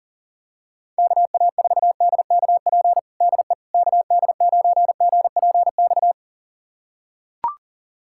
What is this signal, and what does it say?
Morse code, heard as a keyed tone, about 40 wpm, 700 Hz. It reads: KA4DKP DE KD9GPX
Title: USA Callsign DE Callsign 40wpm